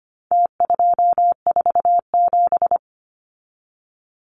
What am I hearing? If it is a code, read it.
Morse code, heard as a keyed tone, about 25 wpm, 700 Hz. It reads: T247
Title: Numbers - Contest Style - 4-Digit 25wpm